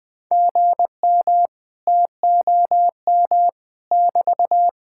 Morse code, heard as a keyed tone, about 20 wpm, 700 Hz. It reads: GM TOM <BT>
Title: Speed-Racing Common QSO Phrases - 17wpm